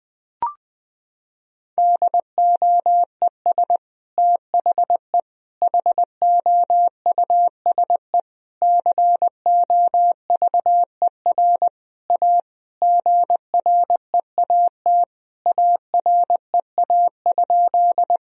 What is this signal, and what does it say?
Morse code, heard as a keyed tone, about 21 wpm, 700 Hz. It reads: DOES THE HOUSE COVER A GREAT AREA?
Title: Sentences from Top 1000 Words - Review Part 2 - 20wpm